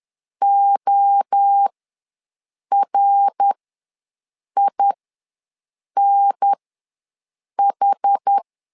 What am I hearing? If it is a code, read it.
Morse code, heard as a keyed tone, about 10 wpm, spaced as for 6 wpm, 800 Hz. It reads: ORINH
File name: LEZIONI / 3 MWRO / 3 MWRO ws 10wpm 2310